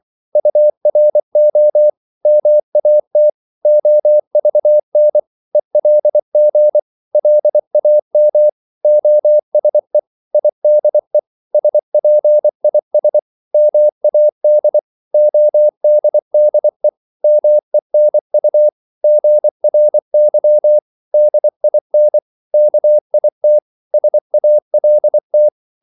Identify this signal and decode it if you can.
Morse code, heard as a keyed tone, about 24 wpm, 600 Hz. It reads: URO MAT OVN ELG LAM OSE IDE SPIS MAD ODDE MENU GRY DIN KIT SALT
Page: Korte danske ord 24wpm